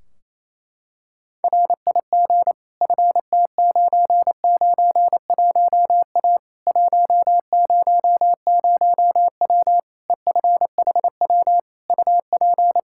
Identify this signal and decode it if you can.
Morse code, heard as a keyed tone, about 28 wpm, 700 Hz. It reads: RIG FT991A 100W EFHW UP
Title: Practice QSO - Exchange 2 - Info - 28wpm